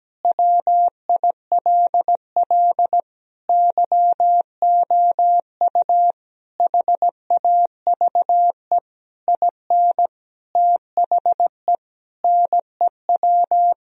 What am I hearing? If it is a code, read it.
Morse code, heard as a keyed tone, about 17 wpm, 700 Hz. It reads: WILL YOU HAVE IN THE NEW
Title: Sentences from Top 1000 Words - Review Part 3 - 17wpm